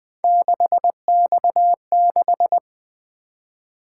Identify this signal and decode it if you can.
Morse code, heard as a keyed tone, about 20 wpm, 700 Hz. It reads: 6X6
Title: Three Letter-Number Combinations - 20wpm